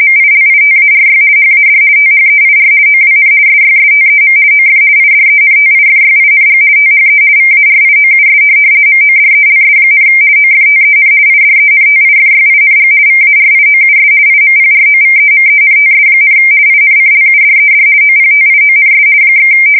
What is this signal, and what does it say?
RTTY, about 50 baud. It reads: MILITARY HAD THEIR OWN SYSTEM OF IDENTIFYING THE VARIOUS MODELS, OFTEN IDENTIFYING VARIOUS IMPROVEMENTS, INCLUDED OPTIONS /
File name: fsk170-50B / teletype-corp